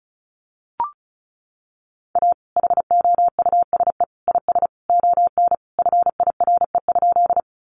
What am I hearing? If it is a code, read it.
Morse code, heard as a keyed tone, about 34 wpm, 700 Hz. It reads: A HOUSE IS ON FIRE?
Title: Sentences from Top 400 Words 35wpm